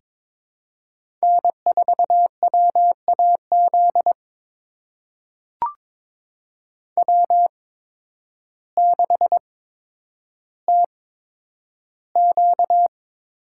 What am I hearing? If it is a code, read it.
Morse code, heard as a keyed tone, about 21 wpm, spaced as for 10 wpm, 700 Hz. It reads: N4WAZ W6TQ
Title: USA Call Signs - 5 Characters - 8x Character Spacing - 22wpm